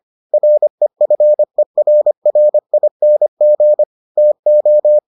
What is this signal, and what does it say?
Morse code, heard as a keyed tone, about 25 wpm, 600 Hz. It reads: REFERRING TO